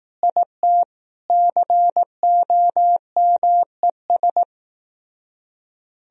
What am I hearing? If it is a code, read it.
Morse code, heard as a keyed tone, about 16 wpm, 700 Hz. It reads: IT COMES
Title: Speed-Racing - Sets of 3 Words 15wpm